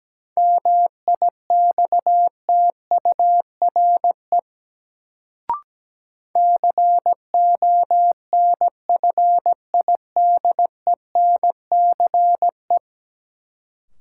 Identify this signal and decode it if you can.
Morse code, heard as a keyed tone, about 17 wpm, 700 Hz. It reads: MIXTURE CONFIDENCE
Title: Sentences from Top 1500 Words - New Word Drill - 17wpm